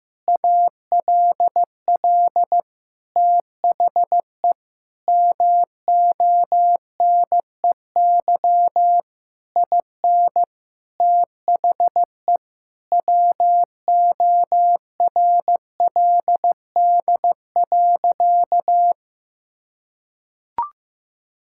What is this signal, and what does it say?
Morse code, heard as a keyed tone, about 15 wpm, 700 Hz. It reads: ALL THE MONEY IN THE WORLD.
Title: Sentences from Top 1000 Words - Review Part 6 - 15wpm